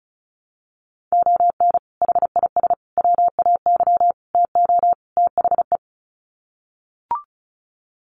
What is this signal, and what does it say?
Morse code, heard as a keyed tone, about 34 wpm, 700 Hz. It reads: ON HIS WAY TO THE
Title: Sets of 5 Words Encore2 35wpm